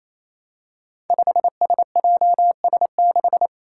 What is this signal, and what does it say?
Morse code, heard as a keyed tone, about 27 wpm, 700 Hz. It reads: 5SJS6
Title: Five Challenging Letter-Number Combinations 28wpm